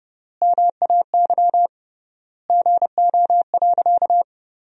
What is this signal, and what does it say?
Morse code, heard as a keyed tone, about 32 wpm, 700 Hz. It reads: MAY GO.
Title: 3x Word-Spacing - Sentences from Top 100 Words - 30wpm